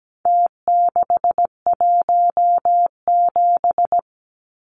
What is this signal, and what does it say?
Morse code, heard as a keyed tone, about 17 wpm, 700 Hz. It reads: T617
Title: Numbers - Contest Style - 4-Digit 17wpm